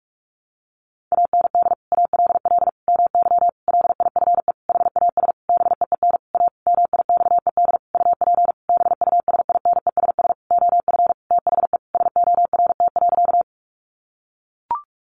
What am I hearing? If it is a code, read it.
Morse code, heard as a keyed tone, about 44 wpm, 700 Hz. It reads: AND ALL MY LIFE HAS BEEN A MIXED UP BUSINESS OF THE SORT.
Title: Sentences from Top 1500 Words 45wpm